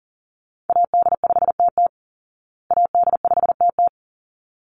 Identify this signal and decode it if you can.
Morse code, heard as a keyed tone, about 40 wpm, 700 Hz. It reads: AD5TT AD5TT
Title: Call Signs Repeated 3x - 40wpm